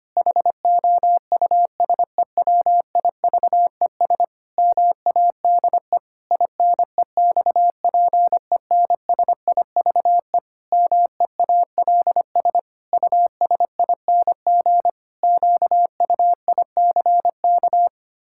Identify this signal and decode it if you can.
Morse code, heard as a keyed tone, about 25 wpm, 700 Hz. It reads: HOUSEWIVES MADE INEXPENSIVE MEALS USING QUICK
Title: Pangram Challenge v2 - 25wpm